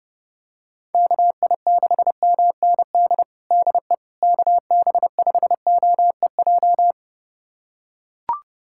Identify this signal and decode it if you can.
Morse code, heard as a keyed tone, about 32 wpm, 700 Hz. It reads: KI6MND DE KB5OEJ
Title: USA Callsign DE Callsign 30wpm